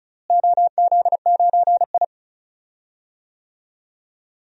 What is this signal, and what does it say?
Morse code, heard as a keyed tone, about 36 wpm, 700 Hz. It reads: OZ9I
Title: DXCC-100 Call Signs - 4 Characters - Repeated 3x - 35wpm